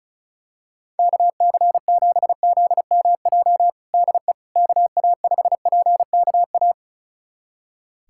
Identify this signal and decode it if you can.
Morse code, heard as a keyed tone, about 36 wpm, 700 Hz. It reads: KC7ZMJ DE KA5PKA
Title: USA Callsign DE Callsign - 6 Characters - 35wpm